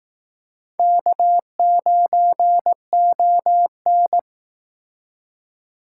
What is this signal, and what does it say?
Morse code, heard as a keyed tone, about 18 wpm, 700 Hz. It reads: K9ON
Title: Speed Racing - USA 1x2 Call Signs 15wpm